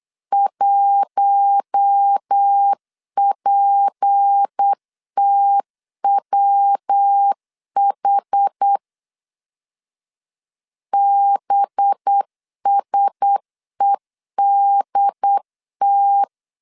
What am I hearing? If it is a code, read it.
Morse code, heard as a keyed tone, about 8 wpm, 800 Hz. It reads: 1PTWH BSEDT
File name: LEZIONI / 6 PKX0 / 6 P 8wpm 3025